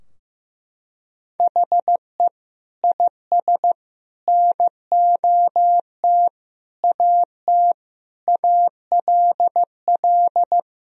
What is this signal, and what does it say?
Morse code, heard as a keyed tone, about 15 wpm, 700 Hz. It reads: HE IS NOT AT ALL
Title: Sentences from Top 1000 Words - Review Part 4 - 15wpm